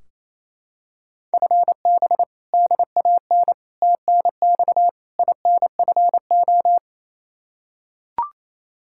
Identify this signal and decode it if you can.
Morse code, heard as a keyed tone, about 28 wpm, 700 Hz. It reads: FB DAN TNX INFO
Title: Practice QSO - Exchange 3 - More Info - 28wpm